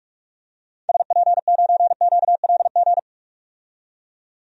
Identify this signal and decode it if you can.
Morse code, heard as a keyed tone, about 44 wpm, 700 Hz. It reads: SP9QLG
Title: DXCC-100 Call Signs - 6 Characters - 45wpm